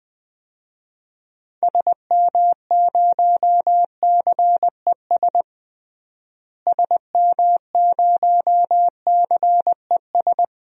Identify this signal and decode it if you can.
Morse code, heard as a keyed tone, about 20 wpm, 700 Hz. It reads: SM0CES SM0CES
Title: DXCC-100 Call Signs - 6 Characters - Repeated 2x - 20wpm